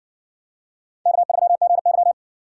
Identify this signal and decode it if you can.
Morse code, heard as a keyed tone, about 60 wpm, 700 Hz. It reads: K3GY